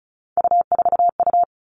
Morse code, heard as a keyed tone, about 36 wpm, 700 Hz. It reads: U4U
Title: Three Challenging Letter-Number Combinations 35wpm